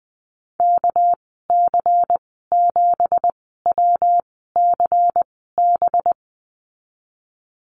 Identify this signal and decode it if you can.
Morse code, heard as a keyed tone, about 20 wpm, spaced as for 15 wpm, 700 Hz. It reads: KC7WCB
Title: 200 Lesson 32g - Letter k - Call Signs 20wpm